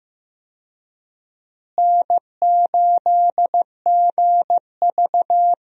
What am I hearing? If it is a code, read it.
Morse code, heard as a keyed tone, about 15 wpm, 700 Hz. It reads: N8GV
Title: USA Call Signs - 4 Characters - Repeated 2x - 15wpm